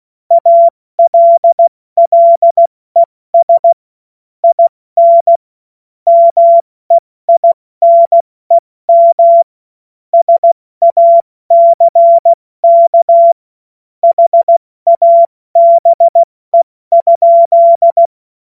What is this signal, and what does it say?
Morse code, heard as a keyed tone, about 16 wpm, 680 Hz. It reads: ALLES IN MEINEM SACK HABE?
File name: Morsekurs / Witze / Witz 24. Januar 2023 70 F80